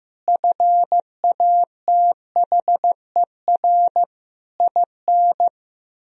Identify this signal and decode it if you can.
Morse code, heard as a keyed tone, about 15 wpm, 700 Hz. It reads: FATHER IN
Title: Sentences from Top 1000 Words - Review Part 4 - 15wpm